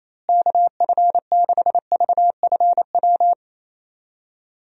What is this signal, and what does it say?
Morse code, heard as a keyed tone, about 28 wpm, 700 Hz. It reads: KF6VFW
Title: USA Call Signs - 6 Characters - Repeated 3x - 28wpm